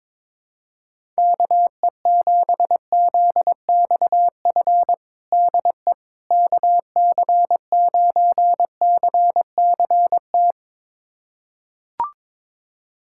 Morse code, heard as a keyed tone, about 22 wpm, 700 Hz. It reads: KE7ZXF DE KC9CCT
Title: USA Callsign DE Callsign - 6 Characters - 22wpm